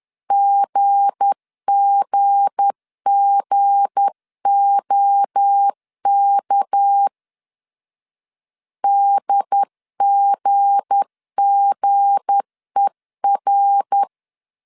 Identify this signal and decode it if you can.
Morse code, heard as a keyed tone, about 10 wpm, 800 Hz. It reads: GGGOK DGGER
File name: LEZIONI / 7 GF9Z / 7 G 10wpm 2501